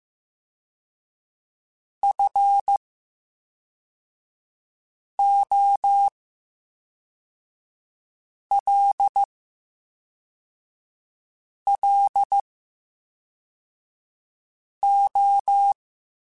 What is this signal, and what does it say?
Morse code, heard as a keyed tone, about 15 wpm, 800 Hz. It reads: F O L L O
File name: Special INGs / Special INGs-mp3s / VerySlowToFaster Ing @10